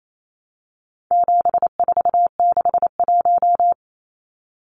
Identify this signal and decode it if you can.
Morse code, heard as a keyed tone, about 28 wpm, 700 Hz. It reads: 7461
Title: Numbers - 4-Digit 28wpm